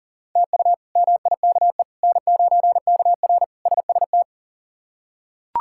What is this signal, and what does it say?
Morse code, heard as a keyed tone, about 40 wpm, 700 Hz. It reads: TU MIKE N9KR SST
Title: K1USN SST QSO Practice - 40wpm